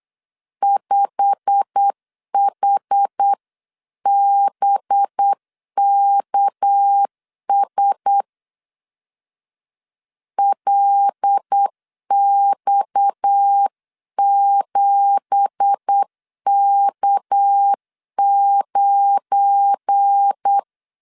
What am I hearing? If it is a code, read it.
Morse code, heard as a keyed tone, about 8 wpm, 800 Hz. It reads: 5HBKS LX7K9
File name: LEZIONI / 9 2867 8wpm 3440